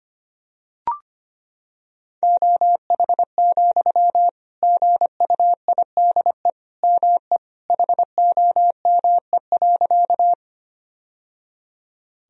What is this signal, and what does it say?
Morse code, heard as a keyed tone, about 25 wpm, 700 Hz. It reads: OH, GUIDE ME HOME.
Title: Sentences from Top 1000 Words - Review Part 3 - 25wpm